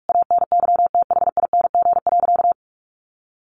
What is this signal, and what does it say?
Morse code, heard as a keyed tone, about 44 wpm, 700 Hz. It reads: ANYTHING.
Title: Sentences from Top 1100 Words 45wpm